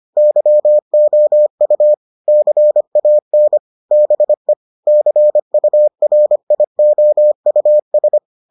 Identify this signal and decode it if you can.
Morse code, heard as a keyed tone, about 25 wpm, 600 Hz. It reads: YOU CAN BE CURIOUS